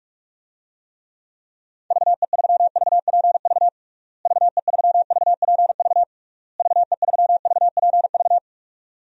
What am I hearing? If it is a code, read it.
Morse code, heard as a keyed tone, about 46 wpm, 700 Hz. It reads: VE3VPV VE3VPV VE3VPV
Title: DXCC-100 Call Signs - 6 Characters - Repeated 3x - 45wpm